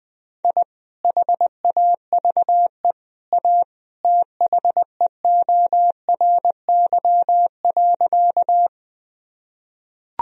Morse code, heard as a keyed tone, about 20 wpm, 700 Hz. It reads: I HAVE A THEORY.
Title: Sentences from Top 1000 Words - Review Part 3 - 20wpm